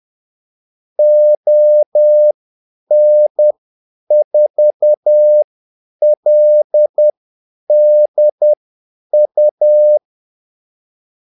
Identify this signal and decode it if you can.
Morse code, heard as a keyed tone, about 10 wpm, spaced as for 8 wpm, 600 Hz. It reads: ON4LDU
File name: Callsigns-10-8wpm